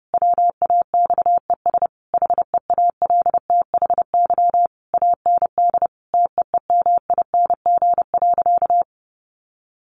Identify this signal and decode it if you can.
Morse code, heard as a keyed tone, about 30 wpm, 700 Hz. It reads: WAXES HEALTHY AND TEEMING.
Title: A Sound Of Thunder 30wpm